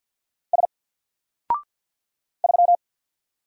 Morse code, heard as a keyed tone, about 50 wpm, 700 Hz. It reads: S 3
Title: Single Letter-Number 50wpm